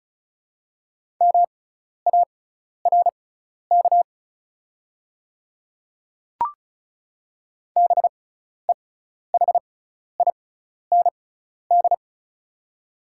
Morse code, heard as a keyed tone, about 34 wpm, spaced as for 12 wpm, 690 Hz. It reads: MARK BEHIND
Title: 6x Character Spacing - Top 500 Words - 35wpm